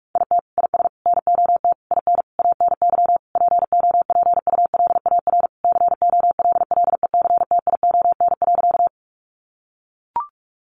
Morse code, heard as a keyed tone, about 44 wpm, 700 Hz. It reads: IT IS NOT IN ANY POPULAR COLLECTION.
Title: Sentences from Top 1200 Words 45wpm